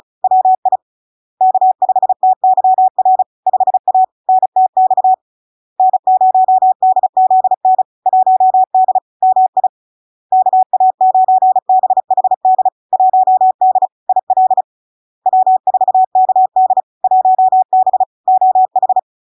Morse code, heard as a keyed tone, about 36 wpm, 750 Hz. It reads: WI K5TYR 5A NTX N0DZN 1D MI KA9BHD 1D IL W4KD 1B OH
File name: LARK-FD-2022-Log-in-CW-35wpm-750-Hz